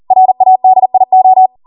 Morse code, heard as a keyed tone, about 40 wpm, 750 Hz. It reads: RADIO